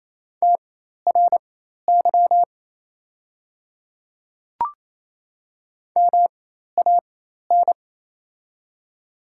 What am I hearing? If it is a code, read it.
Morse code, heard as a keyed tone, about 28 wpm, spaced as for 13 wpm, 700 Hz. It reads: TRY MAN
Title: 4x Character Spacing - Top 200 Words - 28wpm